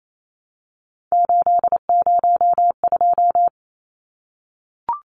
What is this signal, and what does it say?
Morse code, heard as a keyed tone, about 28 wpm, 700 Hz. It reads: 802
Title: Numbers - 3-Digit 28wpm